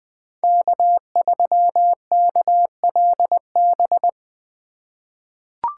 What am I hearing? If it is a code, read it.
Morse code, heard as a keyed tone, about 20 wpm, 700 Hz. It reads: K3KLB
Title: 2023 POTA Activator Call Signs - 20wpm